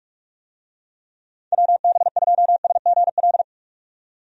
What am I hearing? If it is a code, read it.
Morse code, heard as a keyed tone, about 46 wpm, 700 Hz. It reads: WB2SGL
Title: USA Call Signs - 6 Characters - 45wpm